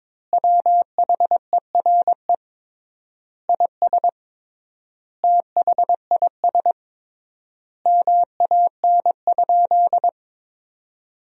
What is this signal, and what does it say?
Morse code, heard as a keyed tone, about 21 wpm, 700 Hz. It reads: WHERE IS THIS MAN?
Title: 3x Word-Spacing - Sentences from Top 200 Words - 22wpm